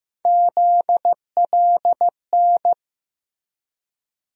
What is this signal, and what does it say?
Morse code, heard as a keyed tone, about 15 wpm, 700 Hz. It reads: ZLN
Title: USA Call Sign Postfixes - 3 Letters 15wpm